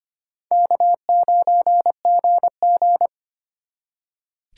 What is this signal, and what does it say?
Morse code, heard as a keyed tone, about 32 wpm, 700 Hz. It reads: K9GG